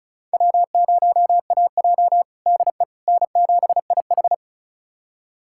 Morse code, heard as a keyed tone, about 34 wpm, 700 Hz. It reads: W0AJ DE N7IH
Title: USA Callsign DE Callsign - 4 Characters - 35wpm